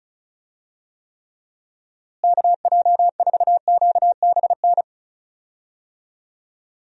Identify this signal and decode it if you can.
Morse code, heard as a keyed tone, about 34 wpm, 700 Hz. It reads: KJ4QBN
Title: USA Call Signs - 6 Characters - Repeated 2x - 35wpm